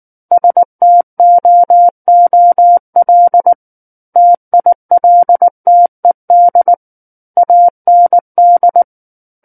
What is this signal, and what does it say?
Morse code, heard as a keyed tone, about 19 wpm, 700 Hz. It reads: STOOL TILTED AND
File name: Book One / 10-Chapter 10 - 19 WPM